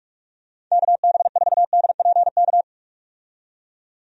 Morse code, heard as a keyed tone, about 46 wpm, 700 Hz. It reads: KB4DPK
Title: USA Call Signs - 6 Characters - 45wpm